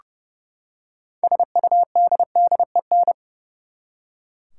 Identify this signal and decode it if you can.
Morse code, heard as a keyed tone, about 30 wpm, 700 Hz. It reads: SUDDEN
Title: Six Letter Words - 30wpm